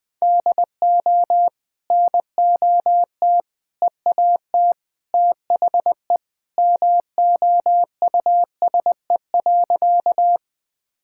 Morse code, heard as a keyed tone, about 20 wpm, 700 Hz. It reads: DO NOT EAT THE MOUSE.
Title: Sentences from Top 1000 Words - Review Part 5 - 20wpm